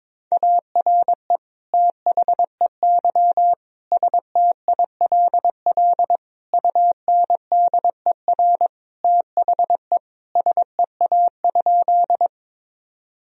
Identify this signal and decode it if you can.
Morse code, heard as a keyed tone, about 22 wpm, 700 Hz. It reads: ARE THEY STILL UNDER THE SEA?
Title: Sentences from Top 1000 Words - Review Part 1 - 22wpm